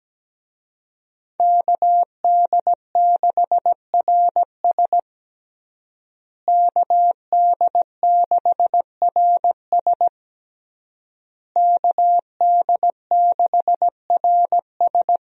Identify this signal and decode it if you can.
Morse code, heard as a keyed tone, about 17 wpm, 700 Hz. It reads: KD6RS KD6RS KD6RS
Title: USA Call Signs - 5 Characters - Repeated 3x - 17wpm